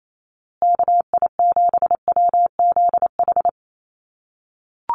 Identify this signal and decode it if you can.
Morse code, heard as a keyed tone, about 28 wpm, 700 Hz. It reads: KI7WZH
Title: USA Call Signs Repeated 2x 28wpm